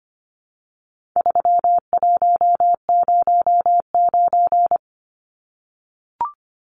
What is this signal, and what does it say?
Morse code, heard as a keyed tone, about 25 wpm, 700 Hz. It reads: 3109
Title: Numbers - 4-Digit 25wpm